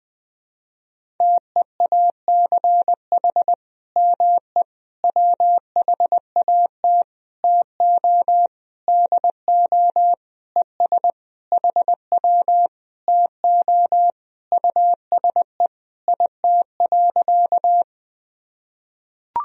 Morse code, heard as a keyed tone, about 21 wpm, 700 Hz. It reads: TEACH ME WHAT TO DO ES HW TO USE IT.
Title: Sentences from Top 500 Words and Abbreviations 20wpm